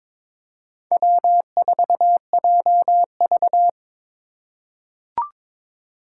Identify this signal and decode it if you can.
Morse code, heard as a keyed tone, about 22 wpm, 700 Hz. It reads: W4JV
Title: USA Call Signs - 4x Character Spacing - 22wpm